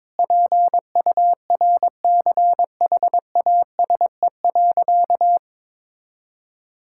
Morse code, heard as a keyed tone, about 22 wpm, 700 Hz. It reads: PURCHASE.
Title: Sentences from Top 1000 Words - Review Part 5 - 22wpm